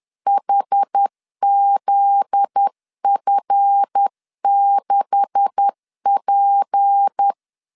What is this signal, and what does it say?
Morse code, heard as a keyed tone, about 10 wpm, 800 Hz. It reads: HZF6P
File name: LEZIONI / 9 286 10wpm 2735